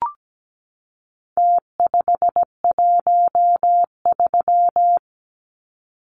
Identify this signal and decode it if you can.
Morse code, heard as a keyed tone, about 17 wpm, 700 Hz. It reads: T513
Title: Numbers - Contest Style - 4-Digit 17wpm